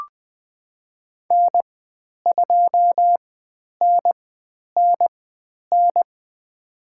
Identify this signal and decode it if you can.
Morse code, heard as a keyed tone, about 20 wpm, spaced as for 10 wpm, 700 Hz. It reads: N2NNN
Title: 102 Lesson 17g - Letter u - Call Signs 20wpm